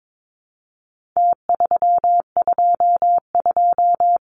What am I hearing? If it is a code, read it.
Morse code, heard as a keyed tone, about 22 wpm, 700 Hz. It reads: T322
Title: Numbers - Contest Style - 4-Digit 22wpm